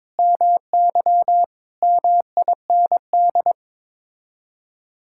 Morse code, heard as a keyed tone, about 22 wpm, 700 Hz. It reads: MY MIND
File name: practice / 22wpm / Sets of 7 Words - 22wpm